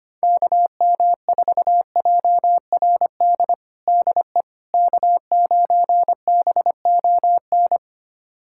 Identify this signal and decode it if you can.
Morse code, heard as a keyed tone, about 25 wpm, 700 Hz. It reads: KM4JRD DE K9BON
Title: USA Callsign DE Callsign 25wpm